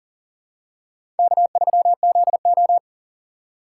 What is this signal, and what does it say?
Morse code, heard as a keyed tone, about 40 wpm, 700 Hz. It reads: K3ZO
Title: USA Call Signs - 4 Characters - 4x Character Spacing - 40wpm